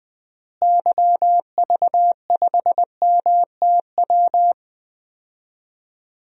Morse code, heard as a keyed tone, about 20 wpm, 700 Hz. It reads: YV5MTW
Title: DXCC-100 Call Signs - 6 Characters - Repeated 2x - 20wpm